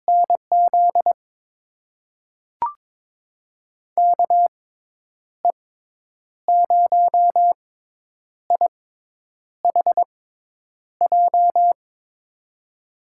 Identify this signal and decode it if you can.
Morse code, heard as a keyed tone, about 22 wpm, spaced as for 9 wpm, 700 Hz. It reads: NZ KE0IHJ
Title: USA Call Signs - 6 Characters - 6x Character Spacing - 22wpm